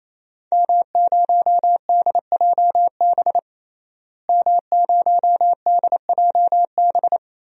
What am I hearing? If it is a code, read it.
Morse code, heard as a keyed tone, about 28 wpm, 700 Hz. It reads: M0DJB M0DJB
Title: DXCC-100 Call Signs - 5 Characters - Repeated 3x - 28wpm